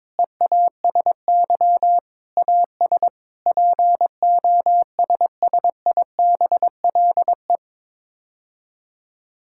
Morse code, heard as a keyed tone, about 22 wpm, 700 Hz. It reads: EASY AS POSSIBLE
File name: practice / 22wpm / Sets of 7 Words - 22wpm